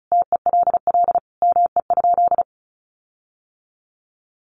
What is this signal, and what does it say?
Morse code, heard as a keyed tone, about 34 wpm, 700 Hz. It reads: TELL ME?
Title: Sentences from Top 1300 Words 35wpm